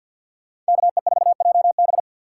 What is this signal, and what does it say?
Morse code, heard as a keyed tone, about 64 wpm, 700 Hz. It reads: KE4JB